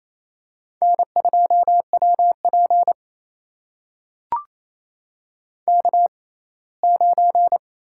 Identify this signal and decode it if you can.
Morse code, heard as a keyed tone, about 28 wpm, spaced as for 19 wpm, 700 Hz. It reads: N2WP K9
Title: USA Call Signs - 4 Characters - 6x Character Spacing - 28wpm